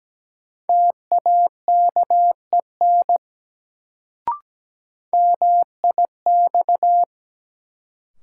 Morse code, heard as a keyed tone, about 17 wpm, 700 Hz. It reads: TAKEN MIX
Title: Sentences from Top 1000 Words - Review Part 4 - New Word Drill - 17wpm